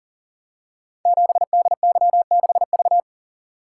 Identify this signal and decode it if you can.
Morse code, heard as a keyed tone, about 40 wpm, 690 Hz. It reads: 7DY6V
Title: Code Groups - 5 Letter:Numbers - 40wpm